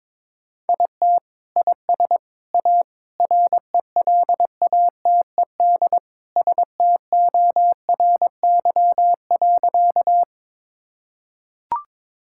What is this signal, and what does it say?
Morse code, heard as a keyed tone, about 22 wpm, 700 Hz. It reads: IT IS A RELATED STORY.
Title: Sentences from Top 1000 Words - Review Part 5 - 22wpm